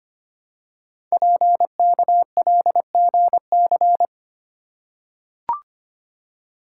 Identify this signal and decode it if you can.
Morse code, heard as a keyed tone, about 25 wpm, 700 Hz. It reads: PKLGC
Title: Code Groups - 5 Letter:Numbers - 25wpm